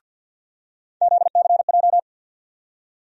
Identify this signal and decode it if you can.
Morse code, heard as a keyed tone, about 49 wpm, 700 Hz. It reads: ZCJ
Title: USA Call Sign Postfixes - 3 Letters 50wpm